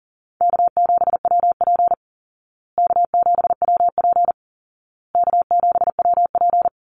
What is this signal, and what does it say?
Morse code, heard as a keyed tone, about 40 wpm, 700 Hz. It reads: K7WP K7WP K7WP
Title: Easy Call Signs Repeated 3x - 40wpm